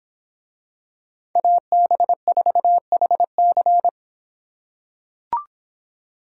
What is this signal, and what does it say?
Morse code, heard as a keyed tone, about 26 wpm, 700 Hz. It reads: AB4HC